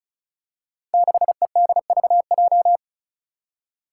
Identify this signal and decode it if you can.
Morse code, heard as a keyed tone, about 36 wpm, 700 Hz. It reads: 6EDVJ
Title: Code Groups - 5 Letter:Numbers - 35wpm